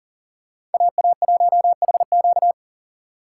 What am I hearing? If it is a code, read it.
Morse code, heard as a keyed tone, about 40 wpm, 700 Hz. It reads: AA1HQ
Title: USA Call Signs - 5 Characters - 6x Character Spacing - 40wpm